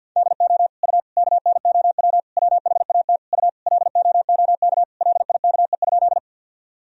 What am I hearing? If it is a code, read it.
Morse code, heard as a keyed tone, about 49 wpm, 700 Hz. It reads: DO U KNOW WHAT U LOOK LIKE?
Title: Sentences from Top 100 Words and Abbreviations 50wpm